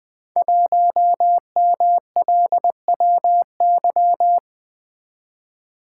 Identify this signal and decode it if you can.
Morse code, heard as a keyed tone, about 20 wpm, 700 Hz. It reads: 1MLWY
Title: Code Groups - 5 Letter:Numbers - 20wpm